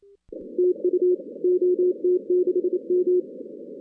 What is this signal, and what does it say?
Morse code, heard as a keyed tone, about 27 wpm, 360 Hz. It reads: TU OT6M
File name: 2016 arrl dx cw / 2016-ARRL-DX-CW-K5TR- - 0-46